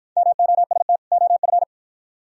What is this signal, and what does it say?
Morse code, heard as a keyed tone, about 51 wpm, 700 Hz. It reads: MOST OF